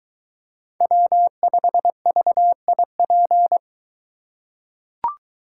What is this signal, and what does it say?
Morse code, heard as a keyed tone, about 22 wpm, 700 Hz. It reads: W5VIP